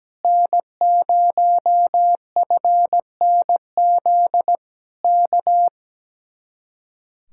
Morse code, heard as a keyed tone, about 17 wpm, 700 Hz. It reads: N0FNZ K
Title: Speed-Racing Common QSO Phrases - 17wpm